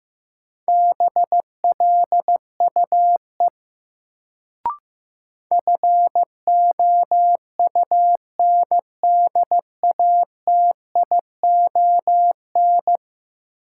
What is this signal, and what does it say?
Morse code, heard as a keyed tone, about 15 wpm, 700 Hz. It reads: BLUE FOUNDATION
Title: Sentences from Top 1200 Words - New Word Drill Alt - 15wpm